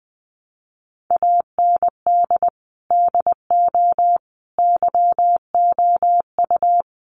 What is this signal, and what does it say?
Morse code, heard as a keyed tone, about 20 wpm, 700 Hz. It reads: AND DO YOU
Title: Sentences from Top 100 Words 20wpm